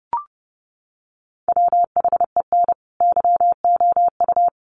Morse code, heard as a keyed tone, about 29 wpm, 700 Hz. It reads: WHEN YOU
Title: Sets of 2 Words 30wpm